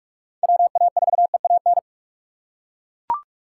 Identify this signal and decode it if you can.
Morse code, heard as a keyed tone, about 41 wpm, 700 Hz. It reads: WA4EAN